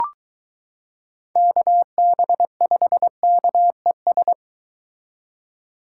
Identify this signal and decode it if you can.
Morse code, heard as a keyed tone, about 23 wpm, 700 Hz. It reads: KB5KES